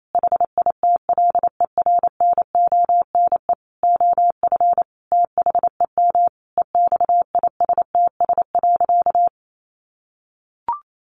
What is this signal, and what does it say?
Morse code, heard as a keyed tone, about 28 wpm, 700 Hz. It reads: HITLER NONE OF THEM EXISTS.
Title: A Sound Of Thunder 28wpm